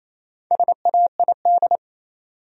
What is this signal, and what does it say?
Morse code, heard as a keyed tone, about 28 wpm, 700 Hz. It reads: SAID
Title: Top 100 Words 28wpm